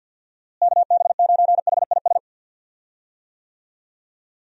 Morse code, heard as a keyed tone, about 52 wpm, 700 Hz. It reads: KB9HIS
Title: USA Call Signs - 6 Characters - Repeated 2x - 50wpm